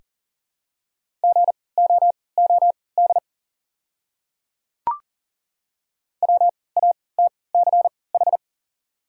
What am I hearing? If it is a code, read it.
Morse code, heard as a keyed tone, about 37 wpm, spaced as for 22 wpm, 700 Hz. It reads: GOOD WATCH
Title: 3x Character Spacing - Top 300 Words - 40wpm